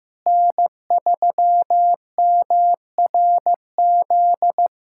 Morse code, heard as a keyed tone, about 15 wpm, 700 Hz. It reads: N3MRZ
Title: USA Callsign DE Callsign - 5 Characters - 15wpm